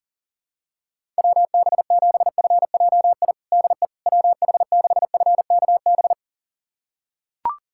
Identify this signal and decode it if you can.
Morse code, heard as a keyed tone, about 40 wpm, 700 Hz. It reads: WB7FJI DE WH6FKB
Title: USA Callsign DE Callsign - 6 Characters - 40wpm